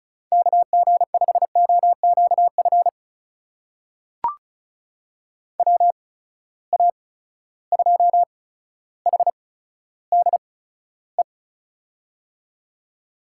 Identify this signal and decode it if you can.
Morse code, heard as a keyed tone, about 34 wpm, spaced as for 15 wpm, 700 Hz. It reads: KG5OQF WA2HDE
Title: USA Call Signs - 6 Characters - 8x Character Spacing - 35wpm